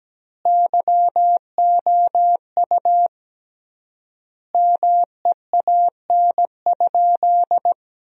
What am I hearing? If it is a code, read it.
Morse code, heard as a keyed tone, about 17 wpm, 700 Hz. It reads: YOU MEAN?
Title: 3x Word-Spacing - Sentences from Top 200 Words - 17wpm